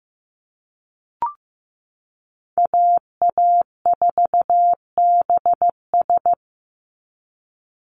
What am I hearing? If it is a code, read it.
Morse code, heard as a keyed tone, about 15 wpm, 700 Hz. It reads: AA4BS
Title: USA Call Signs 15wpm